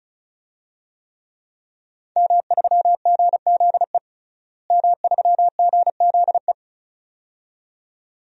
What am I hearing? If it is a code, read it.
Morse code, heard as a keyed tone, about 34 wpm, 700 Hz. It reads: M3GZE M3GZE
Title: DXCC-100 Call Signs - 5 Characters - Repeated 2x - 35wpm